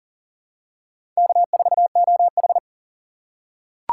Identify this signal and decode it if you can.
Morse code, heard as a keyed tone, about 40 wpm, 700 Hz. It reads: K4OH
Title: USA Call Signs - 4 Characters - 3x Character Spacing - 40wpm